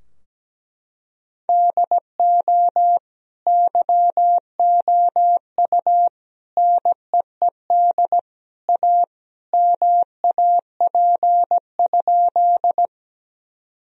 Morse code, heard as a keyed tone, about 17 wpm, 700 Hz. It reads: DO YOU NEED A MAP?
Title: Sentences from Top 1000 Words - Review Part 2 - 17wpm